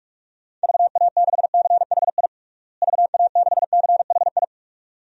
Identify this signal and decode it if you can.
Morse code, heard as a keyed tone, about 46 wpm, 700 Hz. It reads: VA6CHI VA6CHI
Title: DXCC-100 Call Signs - 6 Characters - Repeated 2x - 45wpm